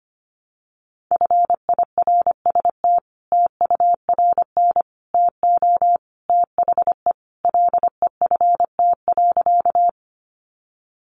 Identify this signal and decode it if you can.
Morse code, heard as a keyed tone, about 25 wpm, 700 Hz. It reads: FIRST TURN TO THE LEFT.
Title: Sentences from Top 300 Words 25wpm